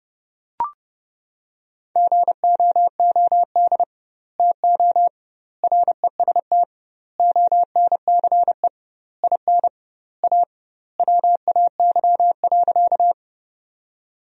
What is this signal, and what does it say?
Morse code, heard as a keyed tone, about 30 wpm, 700 Hz. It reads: GOOD TO REST ONCE IN A WAY.
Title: 2x Word-Spacing - Sentences from Top 500 Words - 30wpm